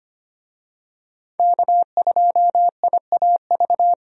Morse code, heard as a keyed tone, about 25 wpm, 700 Hz. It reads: K2IAV
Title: USA Call Signs - 5 Characters - 25wpm